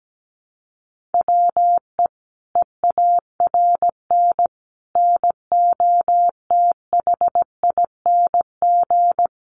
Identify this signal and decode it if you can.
Morse code, heard as a keyed tone, about 17 wpm, 700 Hz. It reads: WE EARN NOTHING
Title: Sentences from Top 1500 Words 17wpm